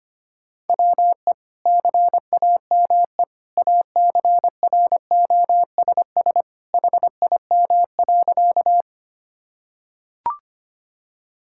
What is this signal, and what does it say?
Morse code, heard as a keyed tone, about 25 wpm, 700 Hz. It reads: WE CAME ACROSS HIM.
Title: Sentences from Top 1000 Words - Review Part 5 - 25wpm